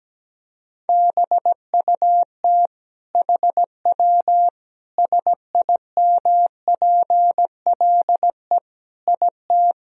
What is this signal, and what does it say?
Morse code, heard as a keyed tone, about 17 wpm, 700 Hz. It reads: BUT HW SIMPLE IT
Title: Sentences from Top 400 Words and Abbreviations 17wpm